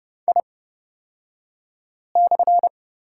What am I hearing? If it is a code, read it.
Morse code, heard as a keyed tone, about 30 wpm, spaced as for 5 wpm, 700 Hz. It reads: I/
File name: Lesson 38d - Letter q - Two Characters 30wpm